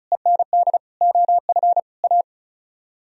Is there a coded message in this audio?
Yes: Morse code from 0.1 to 2.2 s, heard as a keyed tone, about 45 wpm, 700 Hz. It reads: END OF A